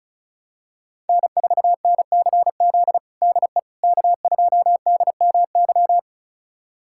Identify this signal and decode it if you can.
Morse code, heard as a keyed tone, about 36 wpm, 700 Hz. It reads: N4NCZ DE K2DMY
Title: USA Callsign DE Callsign - 5 Characters - 35wpm